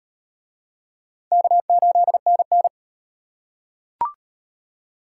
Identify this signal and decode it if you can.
Morse code, heard as a keyed tone, about 34 wpm, 700 Hz. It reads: K8NN